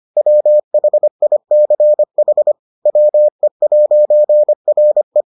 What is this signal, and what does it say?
Morse code, heard as a keyed tone, about 25 wpm, 600 Hz. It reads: WHICH WE'RE